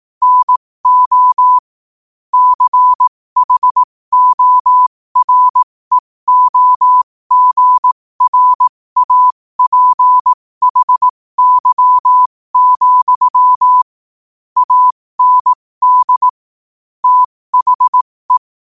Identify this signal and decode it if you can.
Morse code, heard as a keyed tone, about 18 wpm, spaced as for 15 wpm, 1000 Hz. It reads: NO CHOREOGRAPHY, AND THE
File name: qotdfiles / 15wpm